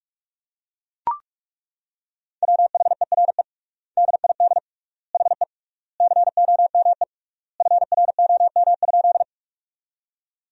Morse code, heard as a keyed tone, about 41 wpm, 700 Hz. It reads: WHERE DID HE COME FROM?
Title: 3x Word-Spacing - Sentences from Top 200 Words - 45wpm